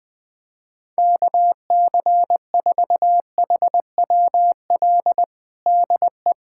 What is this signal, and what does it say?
Morse code, heard as a keyed tone, about 21 wpm, 700 Hz. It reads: KC4HWL DE
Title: USA Callsign DE Callsign 20wpm